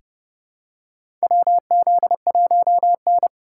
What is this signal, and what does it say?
Morse code, heard as a keyed tone, about 31 wpm, 700 Hz. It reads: WZ1N
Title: USA Call Signs - 4 Characters - 3x Character Spacing - 30wpm